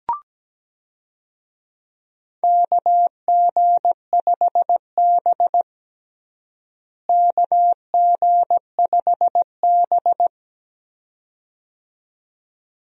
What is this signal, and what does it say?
Morse code, heard as a keyed tone, about 17 wpm, 700 Hz. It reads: KG5B KG5B
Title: USA Call Signs - 4 Characters - Repeated 2x - 17wpm